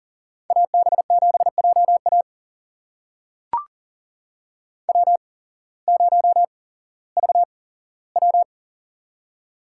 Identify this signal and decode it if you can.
Morse code, heard as a keyed tone, about 39 wpm, spaced as for 19 wpm, 700 Hz. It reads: AB7JA W0VW
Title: USA Call Signs - 8x Character Spacing - 40wpm